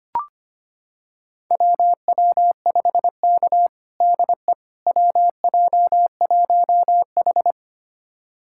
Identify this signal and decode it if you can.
Morse code, heard as a keyed tone, about 25 wpm, 700 Hz. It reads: WW5K DE WJ1H
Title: USA Callsign DE Callsign - 4 Characters - 25wpm